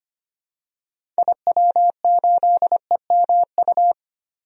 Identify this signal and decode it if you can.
Morse code, heard as a keyed tone, about 25 wpm, 700 Hz. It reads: IW8EMU
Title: DXCC-100 Call Signs - 6 Characters - 25wpm